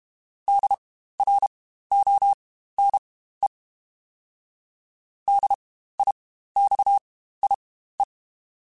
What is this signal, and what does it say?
Morse code, heard as a keyed tone, about 31 wpm, spaced as for 15 wpm, 800 Hz. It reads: DRONE DIXIE